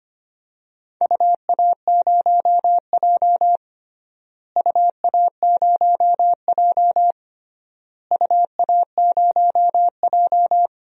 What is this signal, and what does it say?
Morse code, heard as a keyed tone, about 25 wpm, 700 Hz. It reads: UA0J UA0J UA0J
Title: DXCC-100 Call Signs - 4 Characters - Repeated 3x - 25wpm